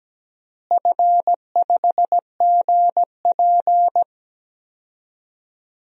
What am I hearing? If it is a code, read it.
Morse code, heard as a keyed tone, about 17 wpm, 700 Hz. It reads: F5GP
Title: DXCC-100 Call Signs - 4 Characters - Repeated 2x - 17wpm